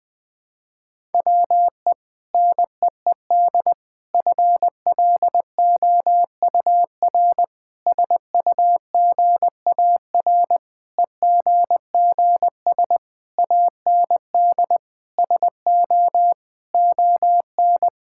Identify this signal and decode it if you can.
Morse code, heard as a keyed tone, about 21 wpm, 700 Hz. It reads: WE NEED FLOUR SUGAR EGGS AND SO ON
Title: Sentences with 6-Word Phrases - 20wpm